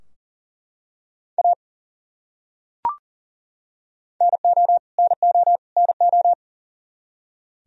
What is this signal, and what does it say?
Morse code, heard as a keyed tone, about 40 wpm, 700 Hz. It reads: A NO NO NO
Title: Top 100 Words Repeated 40wpm